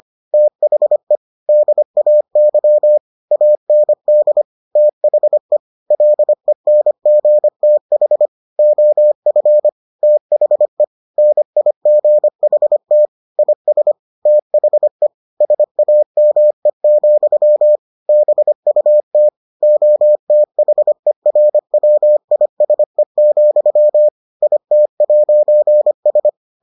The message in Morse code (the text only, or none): THE DAY AND THE LENGTH OF THE NIGHT IS THE SAME, BUT OTHERWISE, IT'S